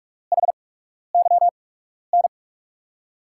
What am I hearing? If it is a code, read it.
Morse code, heard as a keyed tone, about 42 wpm, spaced as for 12 wpm, 700 Hz. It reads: HYN
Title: USA Call Signs - 5 Characters - 8x Character Spacing - 45wpm